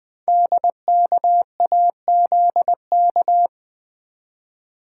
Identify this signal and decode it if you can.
Morse code, heard as a keyed tone, about 20 wpm, 700 Hz. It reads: DKAZK
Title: Code Groups - 5 Letters - 20wpm